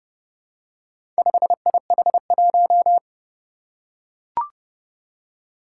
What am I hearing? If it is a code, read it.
Morse code, heard as a keyed tone, about 30 wpm, 700 Hz. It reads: 5IH1
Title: Four Challenging Letter-Number Combinations 30wpm